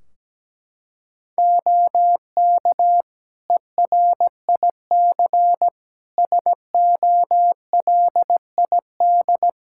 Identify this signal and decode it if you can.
Morse code, heard as a keyed tone, about 17 wpm, 700 Hz. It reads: OK ERIC SOLID
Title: Practice QSO - Exchange 3 - More Info - 17wpm